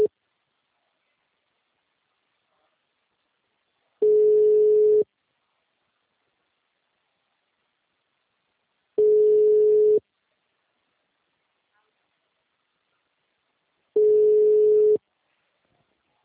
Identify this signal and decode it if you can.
Ringback tone (Europe, 425 Hz), with speech.